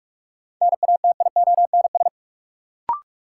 Morse code, heard as a keyed tone, about 42 wpm, 700 Hz. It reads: NATIONS
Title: Sets of 3 Words - New Word Drill - 45wpm